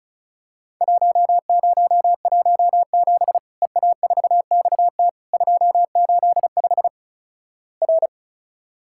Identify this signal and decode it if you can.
Morse code, heard as a keyed tone, about 36 wpm, 700 Hz. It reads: 1017 EA4XT 285 R
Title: Worked All Europe DX Contest - QTC Exchange - 35wpm